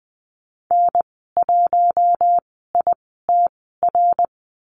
Morse code, heard as a keyed tone, about 20 wpm, spaced as for 14 wpm, 700 Hz. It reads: N1ITR
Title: 055 Lesson 10g - Letter r - Call Signs 20wpm